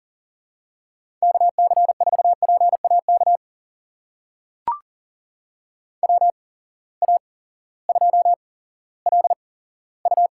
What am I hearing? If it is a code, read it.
Morse code, heard as a keyed tone, about 40 wpm, spaced as for 18 wpm, 700 Hz. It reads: KC4PAK WA2LU
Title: USA Call Signs - 6 Characters - 8x Character Spacing - 40wpm